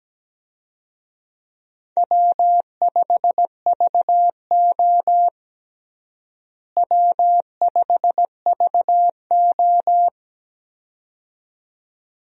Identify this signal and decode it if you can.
Morse code, heard as a keyed tone, about 17 wpm, 700 Hz. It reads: W5VO W5VO
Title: USA Call Signs - 4 Characters - Repeated 2x - 17wpm